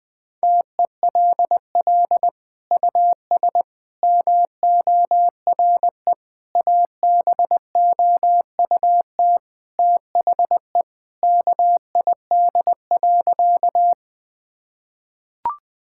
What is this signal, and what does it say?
Morse code, heard as a keyed tone, about 20 wpm, 700 Hz. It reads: TELL US MORE ABOUT THE KID.
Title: Sentences from Top 1600 Words 20wpm